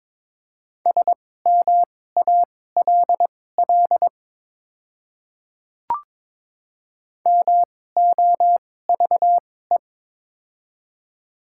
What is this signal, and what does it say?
Morse code, heard as a keyed tone, about 22 wpm, spaced as for 16 wpm, 700 Hz. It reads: SMALL MOVE
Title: 2x Character Spacing - Top 300 Words - 22wpm